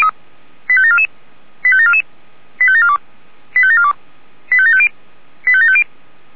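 Five-tone selective call (ZVEI), heard as quick runs of five tones, tones of about 70 ms. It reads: 85744 85744 85742 85742 85750 85750
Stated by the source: ZVEI five-tone selective call 85749 received by SDR, from the Feuerwehr Orlamünde fire brigade server.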